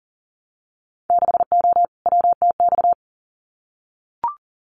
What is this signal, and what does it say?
Morse code, heard as a keyed tone, about 41 wpm, 700 Hz. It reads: 6O WTX
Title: Winter Field Day Exchanges Encore 40wpm